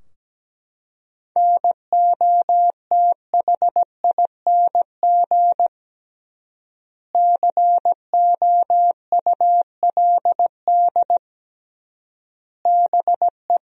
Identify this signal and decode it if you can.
Morse code, heard as a keyed tone, about 17 wpm, 700 Hz. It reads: NOTHING COULD BE
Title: 3x Word-Spacing - Sentences from Top 500 Words - 17wpm